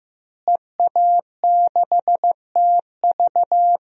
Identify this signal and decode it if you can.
Morse code, heard as a keyed tone, about 15 wpm, 700 Hz. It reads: EA6TV
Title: DXCC-100 Call Signs - 5 Characters - Repeated 2x - 15wpm